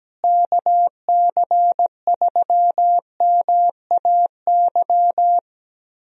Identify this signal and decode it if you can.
Morse code, heard as a keyed tone, about 17 wpm, 700 Hz. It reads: KC3MAY
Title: USA Call Signs - 6 Characters - Repeated 3x - 17wpm